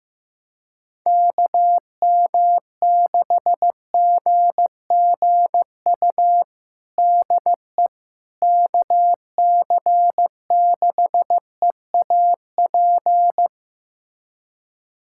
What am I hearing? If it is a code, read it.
Morse code, heard as a keyed tone, about 16 wpm, 700 Hz. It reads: KM6GGU DE KC6EAP
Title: USA Callsign DE Callsign 15wpm